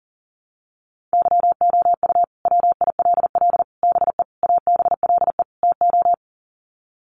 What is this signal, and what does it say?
Morse code, heard as a keyed tone, about 40 wpm, 700 Hz. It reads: YOU WILL BE ABLE TO
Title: Sets of 5 Words Encore2 40wpm